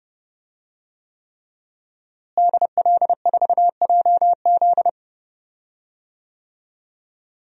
Morse code, heard as a keyed tone, about 30 wpm, 700 Hz. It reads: DL4JZ
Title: DXCC-100 Call Signs - 5 Characters - Repeated 3x - 30wpm